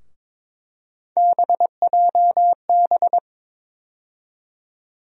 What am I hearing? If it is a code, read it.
Morse code, heard as a keyed tone, about 22 wpm, 700 Hz. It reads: BJB
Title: USA Call Sign Postfixes - 3 Letters 22wpm